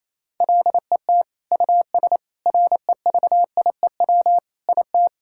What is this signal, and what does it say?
Morse code, heard as a keyed tone, about 27 wpm, 700 Hz. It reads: LET US REVIEW IT
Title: Sentences from Top 1000 Words - Review Part 6 - 28wpm